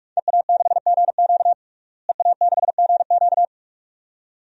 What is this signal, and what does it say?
Morse code, heard as a keyed tone, about 41 wpm, 700 Hz. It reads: EA6GQ EA6GQ
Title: DXCC-100 Call Signs - 5 Characters - Repeated 2x - 45wpm